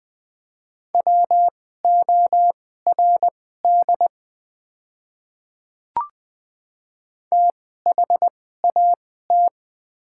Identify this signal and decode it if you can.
Morse code, heard as a keyed tone, about 20 wpm, spaced as for 14 wpm, 700 Hz. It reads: WORD THAT
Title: 2x Character Spacing - Top 200 Words - 20wpm